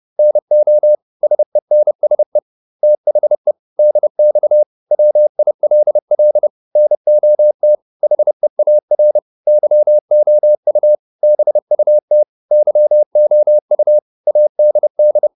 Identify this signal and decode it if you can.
Morse code, heard as a keyed tone, about 30 wpm, 600 Hz. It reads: NO SENSE THE DX WILL NOT HEAR YOU BUT YOU ADD